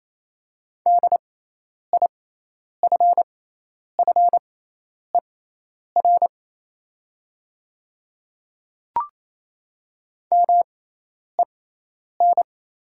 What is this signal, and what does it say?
Morse code, heard as a keyed tone, about 28 wpm, spaced as for 10 wpm, 700 Hz. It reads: DIFFER MEN
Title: 6x Character Spacing - Top 200 Words - 28wpm